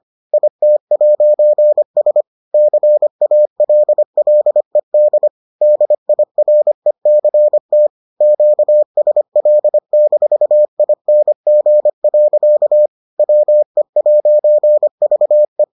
Morse code, heard as a keyed tone, about 25 wpm, 600 Hz. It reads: IT'S CALLED DIRECT QSL-ING. WE'VE